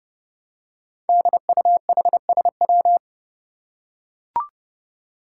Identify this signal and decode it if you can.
Morse code, heard as a keyed tone, about 30 wpm, 700 Hz. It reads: DUHSW
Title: Five Challenging Letter-Number Combinations 30wpm